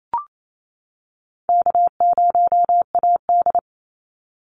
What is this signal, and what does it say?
Morse code, heard as a keyed tone, about 28 wpm, 700 Hz. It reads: K0AD
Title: Easy Call Signs - 28wpm